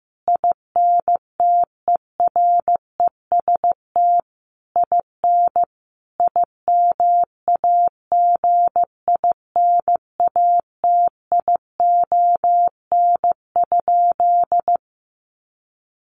Morse code, heard as a keyed tone, about 15 wpm, 700 Hz. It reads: INTEREST IN IMAGINATION?
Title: Sentences from Top 1300 Words 15wpm